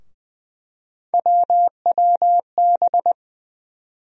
Morse code, heard as a keyed tone, about 20 wpm, 700 Hz. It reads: WWB
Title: USA Call Sign Postfixes - 3 Letters 20wpm